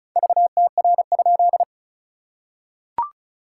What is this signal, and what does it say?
Morse code, heard as a keyed tone, about 34 wpm, 660 Hz. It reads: VTR?
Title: SOTA QSO Practice 35wpm